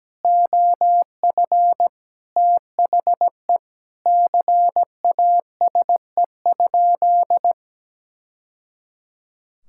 Morse code, heard as a keyed tone, about 17 wpm, 700 Hz. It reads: OF THE CASE?
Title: Sentences from Top 1000 Words - Review Part 5 - 17wpm